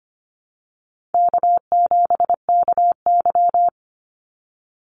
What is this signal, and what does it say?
Morse code, heard as a keyed tone, about 25 wpm, 700 Hz. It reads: K7KY
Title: Easy Call Signs - 25wpm